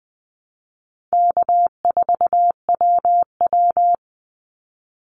Morse code, heard as a keyed tone, about 20 wpm, 700 Hz. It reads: K4WW
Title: Easy Call Signs Repeated 3x - 20wpm